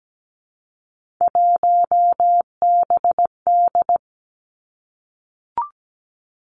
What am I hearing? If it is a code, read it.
Morse code, heard as a keyed tone, about 17 wpm, 700 Hz. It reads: 1BD
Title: Three Challenging Letter-Number Combinations 17wpm